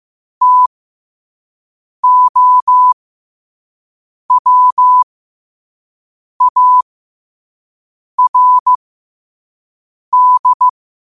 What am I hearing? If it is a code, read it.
Morse code, heard as a keyed tone, about 15 wpm, spaced as for 5 wpm, 1000 Hz. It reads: TOWARD
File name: qotdfiles / 5wpm